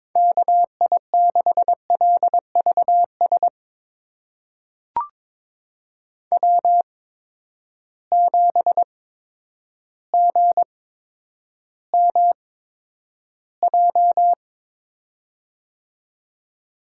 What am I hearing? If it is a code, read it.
Morse code, heard as a keyed tone, about 22 wpm, spaced as for 10 wpm, 700 Hz. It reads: KI6LVS W7GMJ
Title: USA Call Signs - 8x Character Spacing - 22wpm